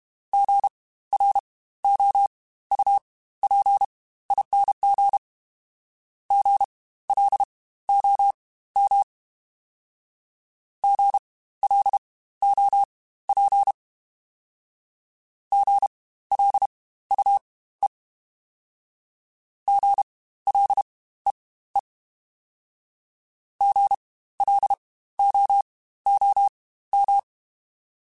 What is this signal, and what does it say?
Morse code, heard as a keyed tone, about 32 wpm, spaced as for 16 wpm, 800 Hz. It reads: GROUPING GLOM GLOP GLUE GLEE GLOOM
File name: Spaced Alphas / Spaced Alphas mp3s / Gs sp @30